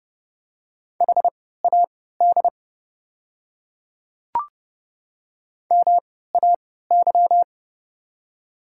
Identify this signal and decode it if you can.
Morse code, heard as a keyed tone, about 30 wpm, spaced as for 17 wpm, 700 Hz. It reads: HAD MAY
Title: 3x Character Spacing - Top 100 Words - 30wpm